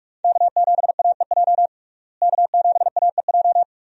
Morse code, heard as a keyed tone, about 46 wpm, 700 Hz. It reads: K7AEJ K7AEJ
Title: USA Call Signs - 5 Characters - Repeated 2x - 45wpm